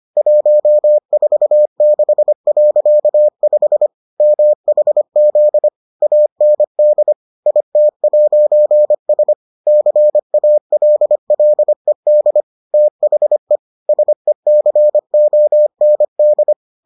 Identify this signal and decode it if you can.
Morse code, heard as a keyed tone, about 25 wpm, 600 Hz. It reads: 146.5 MHZ AND IT'S CALLED THE SECOND